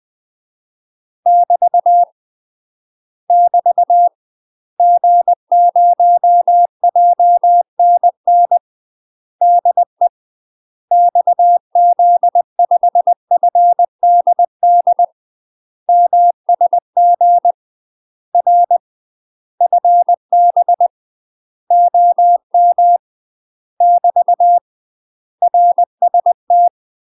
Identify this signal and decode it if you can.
Morse code, heard as a keyed tone, about 20 wpm, 700 Hz. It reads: <BT> <BT> G0JNN DE XZ5FDD MSG R FB OM <BT> RST
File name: UK MASTER QSOs TRAINING / 20WPM / UK MASTER QSOs 0049